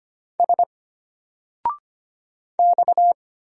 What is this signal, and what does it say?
Morse code, heard as a keyed tone, about 25 wpm, 700 Hz. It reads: S X
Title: Single Character (Letter:Number:Punctuation:Prosign) Alt 25wpm